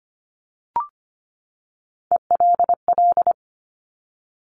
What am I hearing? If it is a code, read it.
Morse code, heard as a keyed tone, about 25 wpm, 700 Hz. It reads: ELL
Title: Top Three Letter Combinations in Words 25wpm